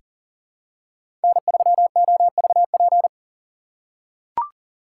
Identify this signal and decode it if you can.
Morse code, heard as a keyed tone, about 40 wpm, 700 Hz. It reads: N3OVP
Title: USA Call Signs - 3x Character Spacing - 40wpm